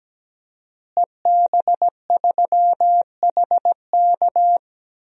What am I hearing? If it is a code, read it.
Morse code, heard as a keyed tone, about 17 wpm, 700 Hz. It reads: EB3HK
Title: DXCC-100 Call Signs - 5 Characters - Repeated 2x - 17wpm